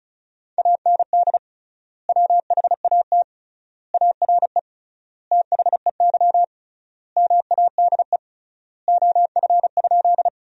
Morse code, heard as a keyed tone, about 35 wpm, 700 Hz. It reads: AND WHAT ARE THEY MADE OF?
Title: 3x Word-Spacing - Sentences from Top 200 Words - 35wpm